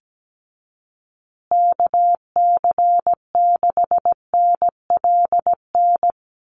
Morse code, heard as a keyed tone, about 17 wpm, 700 Hz. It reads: KC6NLN
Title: USA Call Signs Repeated 3x 17wpm